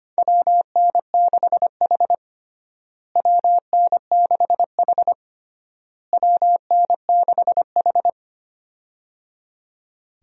Morse code, heard as a keyed tone, about 25 wpm, 700 Hz. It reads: WN6H WN6H WN6H
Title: USA Call Signs - 4 Characters - Repeated 3x - 25wpm